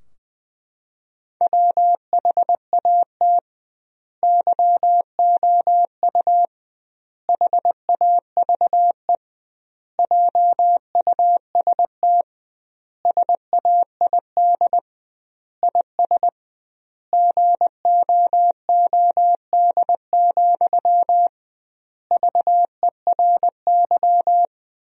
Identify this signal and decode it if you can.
Morse code, heard as a keyed tone, about 20 wpm, 700 Hz. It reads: WHAT YOU HAVE JUST SAID IS GOOD, VERY
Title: 2x Word-Spacing - Sentences from Top 200 Words - 20wpm